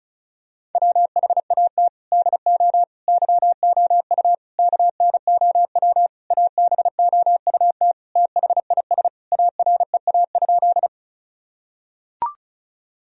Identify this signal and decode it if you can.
Morse code, heard as a keyed tone, about 36 wpm, 700 Hz. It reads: WHAT DO YOU KNOW ABOUT THIS AREA?
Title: Sentences from Top 1000 Words - Review Part 2 - 35wpm